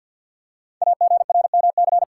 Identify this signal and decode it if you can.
Morse code, heard as a keyed tone, about 49 wpm, 700 Hz. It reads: AGRMC
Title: Code Groups - 5 Letters - 50wpm